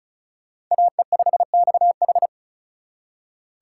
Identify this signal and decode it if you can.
Morse code, heard as a keyed tone, about 40 wpm, 700 Hz. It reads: AE5XH
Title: Speed Racing - USA 2x2 Call Signs 35wpm